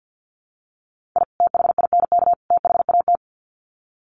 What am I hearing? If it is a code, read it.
Morse code, heard as a keyed tone, about 48 wpm, 700 Hz. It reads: I THINK THAT
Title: Sets of 3 Words 50wpm